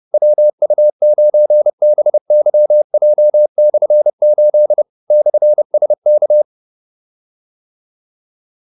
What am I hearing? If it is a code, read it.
Morse code, heard as a keyed tone, about 30 wpm, 600 Hz. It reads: WU9BYJ/8 /SK